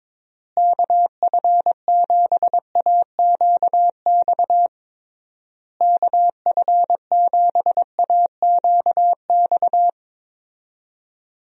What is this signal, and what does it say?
Morse code, heard as a keyed tone, about 22 wpm, 700 Hz. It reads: KF7AQX KF7AQX
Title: USA Call Signs - 6 Characters - Repeated 3x - 22wpm